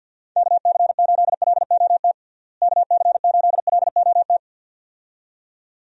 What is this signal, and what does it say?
Morse code, heard as a keyed tone, about 52 wpm, 700 Hz. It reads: KC8LOT KC8LOT
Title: USA Call Signs - 6 Characters - Repeated 2x - 50wpm